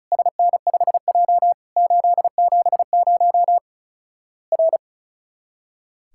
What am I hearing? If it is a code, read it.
Morse code, heard as a keyed tone, about 36 wpm, 700 Hz. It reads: SN5J 870 R
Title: Worked All Europe DX Contest - QTC Exchange - 35wpm